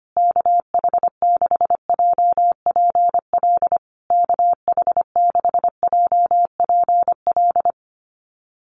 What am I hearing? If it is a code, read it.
Morse code, heard as a keyed tone, about 25 wpm, 700 Hz. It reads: KH6JPL KH6JPL
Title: USA Call Signs Repeated 3x 25wpm